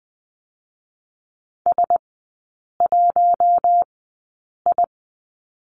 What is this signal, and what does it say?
Morse code, heard as a keyed tone, about 20 wpm, spaced as for 8 wpm, 700 Hz. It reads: S1I
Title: 047 Lesson 9e - Number 4 - Three Characters 20wpm